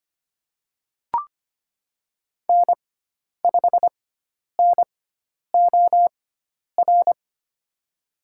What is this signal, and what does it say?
Morse code, heard as a keyed tone, about 25 wpm, spaced as for 10 wpm, 700 Hz. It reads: N5NOR
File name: Lesson 16g - Letter c - Call Signs 25wpm